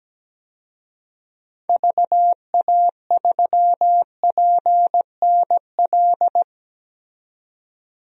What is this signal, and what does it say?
Morse code, heard as a keyed tone, about 17 wpm, 700 Hz. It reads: VA3PNL
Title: DXCC-100 Call Signs - 6 Characters - 17wpm